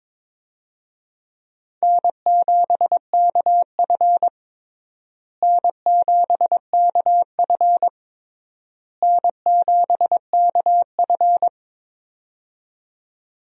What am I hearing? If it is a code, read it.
Morse code, heard as a keyed tone, about 22 wpm, 700 Hz. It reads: N7KF N7KF N7KF
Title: USA Call Signs - 4 Characters - Repeated 3x - 22wpm